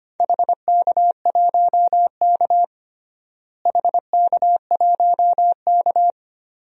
Morse code, heard as a keyed tone, about 25 wpm, 700 Hz. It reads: HK1K HK1K
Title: DXCC-100 Call Signs - 4 Characters - Repeated 2x - 25wpm